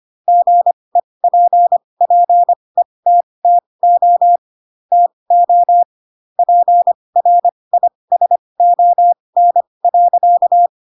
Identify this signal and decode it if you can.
Morse code, heard as a keyed tone, about 25 wpm, spaced as for 20 wpm, 700 Hz. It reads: GEPPETTO TO PRISON.
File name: Pinocchio / Pinocchio ch4 15 25 ik1hge